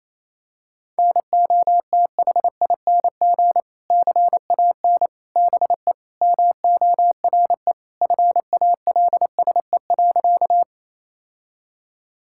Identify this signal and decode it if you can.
Morse code, heard as a keyed tone, about 28 wpm, 700 Hz. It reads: NOTHING CAN BE MORE FALSE.
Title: Sentences from Top 1600 Words 28wpm